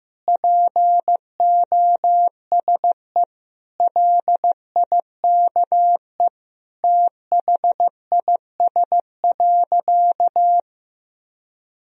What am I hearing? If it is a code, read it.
Morse code, heard as a keyed tone, about 16 wpm, 700 Hz. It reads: POSE LIKE THIS.
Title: Sentences from Top 1000 Words - Review Part 2 - 15wpm